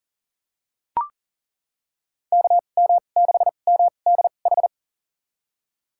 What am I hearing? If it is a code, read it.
Morse code, heard as a keyed tone, about 40 wpm, spaced as for 29 wpm, 700 Hz. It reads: KM6MDH
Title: USA Call Signs - 2x Character Spacing - 40wpm